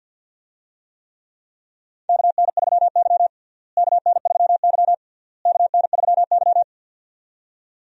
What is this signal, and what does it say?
Morse code, heard as a keyed tone, about 51 wpm, 700 Hz. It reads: KN3Y KN3Y KN3Y
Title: USA Call Signs - 4 Characters - Repeated 3x - 50wpm